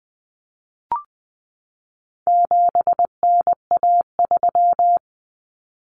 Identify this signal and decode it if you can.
Morse code, heard as a keyed tone, about 20 wpm, 700 Hz. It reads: 7NA3
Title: Common Cut Numbers - 4-Digit 20wpm